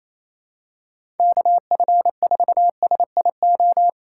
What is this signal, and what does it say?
Morse code, heard as a keyed tone, about 27 wpm, 690 Hz. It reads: KF4SIO
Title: USA Call Signs - 3x Character Spacing - 28wpm